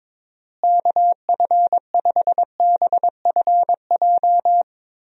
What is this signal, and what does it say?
Morse code, heard as a keyed tone, about 22 wpm, 700 Hz. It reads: KF5BFJ
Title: USA Call Signs - 6 Characters - Repeated 3x - 22wpm